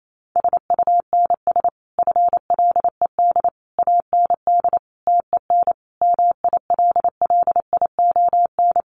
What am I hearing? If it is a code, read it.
Morse code, heard as a keyed tone, about 28 wpm, 700 Hz. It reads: SUNS FLED AND TEN MILLION
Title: A Sound Of Thunder 28wpm